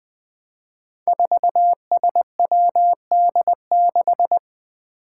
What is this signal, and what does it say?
Morse code, heard as a keyed tone, about 20 wpm, 700 Hz. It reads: 4SWD6
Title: Five Challenging Letter-Number Combinations 20wpm